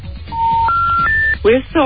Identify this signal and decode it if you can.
Special information tone, with speech.